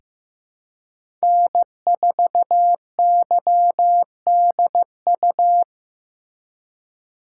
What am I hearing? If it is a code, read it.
Morse code, heard as a keyed tone, about 15 wpm, 700 Hz. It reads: N4YDU
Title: K1USN SST QSO Practice - 15wpm